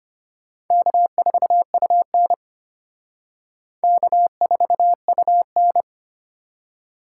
Morse code, heard as a keyed tone, about 31 wpm, 700 Hz. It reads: K4UN K4UN
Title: Speed Racing - USA 1x2 Call Signs 25wpm